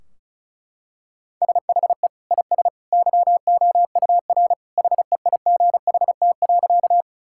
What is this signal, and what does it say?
Morse code, heard as a keyed tone, about 36 wpm, 700 Hz. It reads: SHE IS YOUR HEIGHT.
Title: Sentences from Top 1600 Words 35wpm